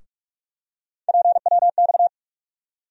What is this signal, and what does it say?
Morse code, heard as a keyed tone, about 42 wpm, 700 Hz. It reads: PWX
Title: USA Call Sign Postfixes - 3 Letters 45wpm